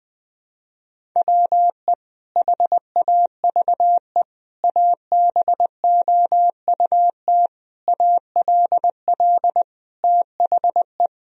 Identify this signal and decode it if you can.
Morse code, heard as a keyed tone, about 20 wpm, 700 Hz. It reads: WE HAVE ABOUT ALL THE
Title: Sentences from Top 1000 Words - Review Part 1 - 20wpm